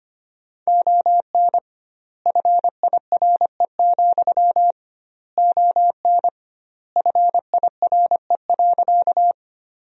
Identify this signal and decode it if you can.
Morse code, heard as a keyed tone, about 25 wpm, 700 Hz. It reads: ON FIRE, ON FIRE.
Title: 2x Word-Spacing - Sentences from Top 400 Words - 25wpm